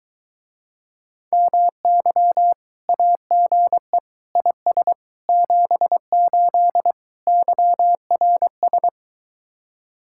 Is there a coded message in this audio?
Yes: Morse code from 1.3 to 8.9 s, heard as a keyed tone, about 24 wpm, 700 Hz. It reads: MY AGE IS 78 YRS